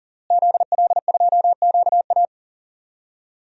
Morse code, heard as a keyed tone, about 41 wpm, 700 Hz. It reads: ZL2QA
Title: DXCC-100 Call Signs - 5 Characters - Repeated 3x - 40wpm